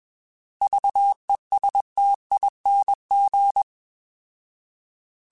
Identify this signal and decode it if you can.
Morse code, heard as a keyed tone, about 21 wpm, 800 Hz. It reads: VESTING
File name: Alphabetical / Alphabetical mp3s / Vs @20